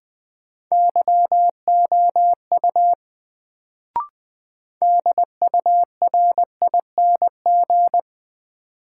Morse code, heard as a keyed tone, about 21 wpm, 700 Hz. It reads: YOU DURING
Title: Sets of 2 Words - New Word Drill - 20wpm